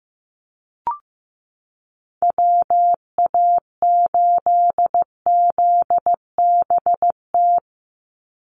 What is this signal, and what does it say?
Morse code, heard as a keyed tone, about 15 wpm, 700 Hz. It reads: WA8ZBT
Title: Call Signs Repeated 3x - 15wpm